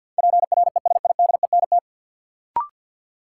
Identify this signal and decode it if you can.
Morse code, heard as a keyed tone, about 53 wpm, 700 Hz. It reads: PRESIDENT
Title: Sets of 2 Words - New Word Drill - 50wpm